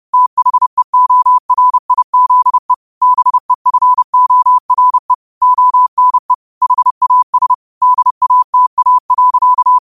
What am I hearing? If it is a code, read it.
Morse code, heard as a keyed tone, about 30 wpm, 1000 Hz. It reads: THEORIZE BEFORE ONE HAS DATA.